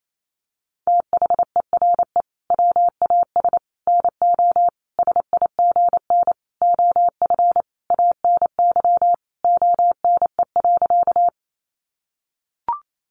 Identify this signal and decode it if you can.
Morse code, heard as a keyed tone, about 28 wpm, 700 Hz. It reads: THERE WAS NO SIGN OF ANY ONE.
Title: Sentences from Top 1000 Words 28wpm